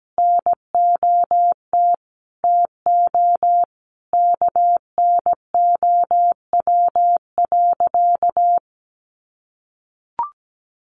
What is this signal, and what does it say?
Morse code, heard as a keyed tone, about 17 wpm, 700 Hz. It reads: NOT TO KNOW.
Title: Sentences from Top 600 Words 17wpm